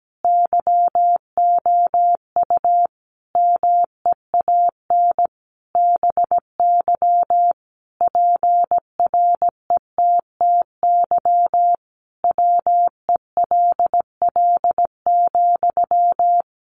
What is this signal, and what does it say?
Morse code, heard as a keyed tone, about 17 wpm, 700 Hz. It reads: YOU MEAN BY PRETTY WELL,
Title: Sentences from Top 1000 Words 17wpm